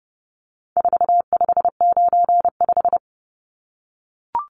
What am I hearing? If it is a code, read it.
Morse code, heard as a keyed tone, about 30 wpm, 700 Hz. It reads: 4595
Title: Numbers - 4-Digit 30wpm